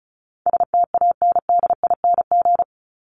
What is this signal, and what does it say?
Morse code, heard as a keyed tone, about 34 wpm, 700 Hz. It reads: STANDING
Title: ING Suffix - Top 100 Words with Base Words 35wpm